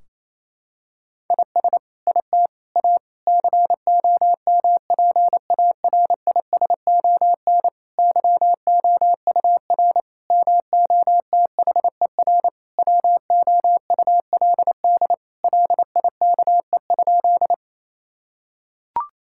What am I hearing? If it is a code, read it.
Morse code, heard as a keyed tone, about 27 wpm, 700 Hz. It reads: IS IT A COMPARISON YOUR MOTHER WOULD LIKE?
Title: Sentences from Top 1600 Words 28wpm